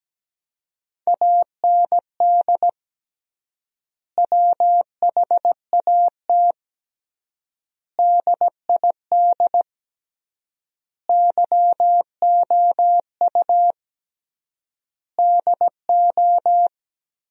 Morse code, heard as a keyed tone, about 17 wpm, 700 Hz. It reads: AND WHAT DID YOU DO
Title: 3x Word-Spacing - Sentences from Top 300 Words - 17wpm